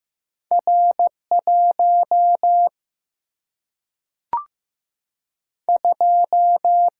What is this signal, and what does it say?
Morse code, heard as a keyed tone, about 15 wpm, 700 Hz. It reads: R 1 2
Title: Single-Letter Single-Number 15wpm